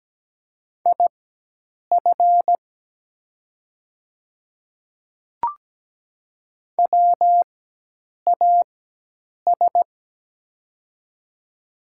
Morse code, heard as a keyed tone, about 17 wpm, spaced as for 8 wpm, 700 Hz. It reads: IF WAS
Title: 4x Character Spacing - Top 100 Words - 17wpm